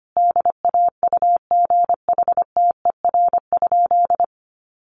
Morse code, heard as a keyed tone, about 25 wpm, 700 Hz. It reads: DAUGHTER?
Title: Sentences from Top 1500 Words 25wpm